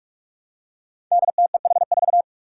Morse code, heard as a keyed tone, about 44 wpm, 700 Hz. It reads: DTEH4
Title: Five Challenging Letter-Number Combinations 45wpm